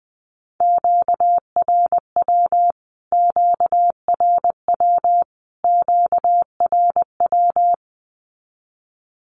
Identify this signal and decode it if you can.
Morse code, heard as a keyed tone, about 20 wpm, 700 Hz. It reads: QRW QRW QRW
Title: Q-Codes Repeated 3x 20wpm